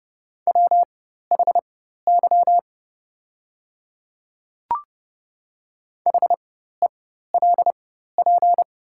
Morse code, heard as a keyed tone, about 30 wpm, spaced as for 14 wpm, 700 Hz. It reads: WHY HELP
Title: 4x Character Spacing - Top 200 Words - 30wpm